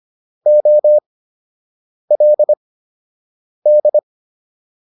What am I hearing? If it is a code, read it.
Morse code, heard as a keyed tone, about 25 wpm, spaced as for 7 wpm, 600 Hz. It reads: OLD